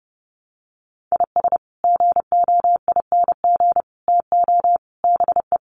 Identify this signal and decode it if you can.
Morse code, heard as a keyed tone, about 30 wpm, 700 Hz. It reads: IS GOING TO BE
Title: Sets of 4 Words 30wpm